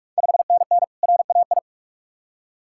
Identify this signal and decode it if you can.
Morse code, heard as a keyed tone, about 44 wpm, 700 Hz. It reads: 5NN RAI
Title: Ontario QSO Party Exchange 45wpm